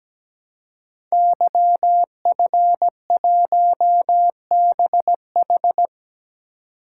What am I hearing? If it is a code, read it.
Morse code, heard as a keyed tone, about 17 wpm, 700 Hz. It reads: YF1BH
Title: Code Groups - 5 Letter:Numbers - 17wpm